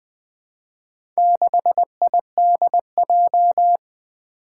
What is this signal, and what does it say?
Morse code, heard as a keyed tone, about 20 wpm, 700 Hz. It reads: 6IDJ
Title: Four Challenging Letter-Number Combinations 20wpm